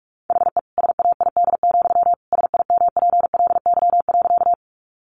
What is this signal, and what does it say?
Morse code, heard as a keyed tone, about 44 wpm, 700 Hz. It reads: HE SAID, SIMPLY.
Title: Sentences from Top 600 Words 45wpm